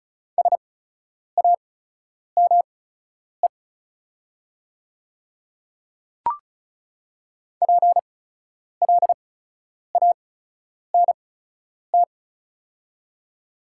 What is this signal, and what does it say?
Morse code, heard as a keyed tone, about 36 wpm, spaced as for 10 wpm, 700 Hz. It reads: SAME PLANT
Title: 8x Character Spacing - Top 500 Words - 35wpm